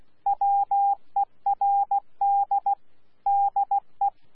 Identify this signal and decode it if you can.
Morse code, heard as a keyed tone, about 16 wpm, 800 Hz. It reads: WERD DE
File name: Z PI4VRZ / PI4VRZ 14-jun-2025